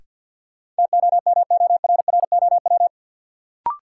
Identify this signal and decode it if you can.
Morse code, heard as a keyed tone, about 53 wpm, 700 Hz. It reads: TOMORROW
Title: Sentences from Top 1100 Words - New Word Drill - 50wpm